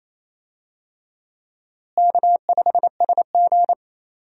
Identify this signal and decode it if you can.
Morse code, heard as a keyed tone, about 28 wpm, 700 Hz. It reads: K5SG
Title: USA Call Signs - 4 Characters - Repeated 3x - 28wpm